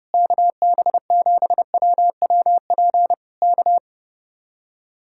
Morse code, heard as a keyed tone, about 28 wpm, 700 Hz. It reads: KB7WWP K
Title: Speed-Racing Common QSO Phrases - 30wpm